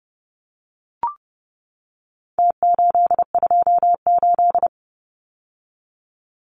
Morse code, heard as a keyed tone, about 30 wpm, 700 Hz. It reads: T828
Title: Numbers - Contest Style - 4-Digit 30wpm